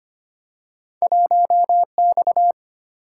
Morse code, heard as a keyed tone, about 25 wpm, 700 Hz. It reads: 1X
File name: Two Letter-Number Combinations - 25wpm